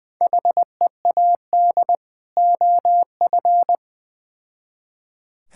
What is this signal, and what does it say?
Morse code, heard as a keyed tone, about 20 wpm, 700 Hz. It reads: HEAD OF
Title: Build A Phrase - 3-Word Phrases - 20wpm